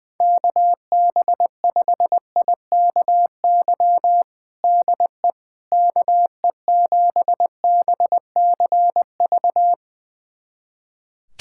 Morse code, heard as a keyed tone, about 20 wpm, 700 Hz. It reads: KB5IKY DE KE7BCV
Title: USA Callsign DE Callsign - 6 Characters - 20wpm